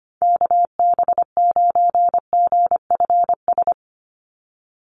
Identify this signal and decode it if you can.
Morse code, heard as a keyed tone, about 25 wpm, 690 Hz. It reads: KB9GFS
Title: USA Call Signs 25wpm